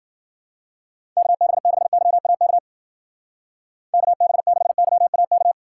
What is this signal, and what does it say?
Morse code, heard as a keyed tone, about 51 wpm, 700 Hz. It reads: KB6YAK KB6YAK